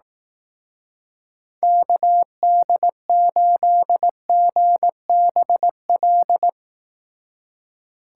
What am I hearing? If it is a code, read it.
Morse code, heard as a keyed tone, about 18 wpm, 700 Hz. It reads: KD8GBL
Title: Speed Racing - USA 2x3 Call Signs 15wpm